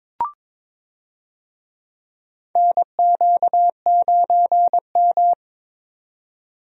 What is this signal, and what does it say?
Morse code, heard as a keyed tone, about 22 wpm, 700 Hz. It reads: NQ9M
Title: USA Call Signs - 4 Characters - 22wpm